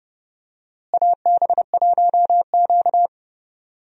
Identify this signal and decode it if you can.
Morse code, heard as a keyed tone, about 32 wpm, 700 Hz. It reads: AB1Q
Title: USA Call Signs - 4 Characters - Repeated 3x - 30wpm